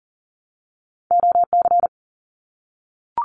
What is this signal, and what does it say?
Morse code, heard as a keyed tone, about 40 wpm, 700 Hz. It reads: OC
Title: Abbreviations 40wpm